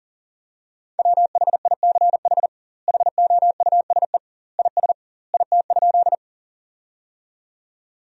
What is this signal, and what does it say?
Morse code, heard as a keyed tone, about 37 wpm, 700 Hz. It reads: WHICH HOUSE IS IT?
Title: 2x Word-Spacing - Sentences from Top 200 Words - 40wpm